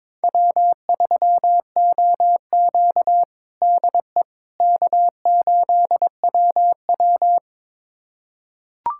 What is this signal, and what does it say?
Morse code, heard as a keyed tone, about 22 wpm, 700 Hz. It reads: W3OQ DE K8WW
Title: USA Callsign DE Callsign - 4 Characters - 22wpm